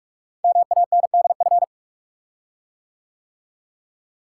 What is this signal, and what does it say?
Morse code, heard as a keyed tone, about 44 wpm, 700 Hz. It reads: MANDF
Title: Code Groups - 5 Letters - 45wpm